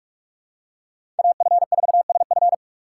Morse code, heard as a keyed tone, about 44 wpm, 690 Hz. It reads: AF4SF
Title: USA Call Signs - 5 Characters - 8x Character Spacing - 45wpm